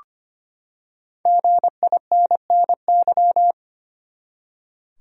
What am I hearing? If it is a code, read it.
Morse code, heard as a keyed tone, about 25 wpm, 700 Hz. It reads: GINNY
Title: USA Common Names - Max 5 Letters - 25wpm